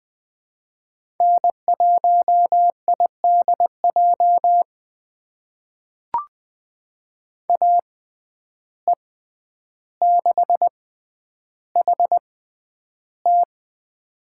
Morse code, heard as a keyed tone, about 20 wpm, spaced as for 10 wpm, 700 Hz. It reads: N1IDJ AE6HT
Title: USA Call Signs - 5 Characters - 6x Character Spacing - 20wpm